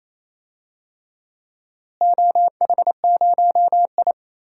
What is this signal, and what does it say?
Morse code, heard as a keyed tone, about 28 wpm, 700 Hz. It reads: OH0I
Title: DXCC-100 Call Signs - 4 Characters - Repeated 3x - 28wpm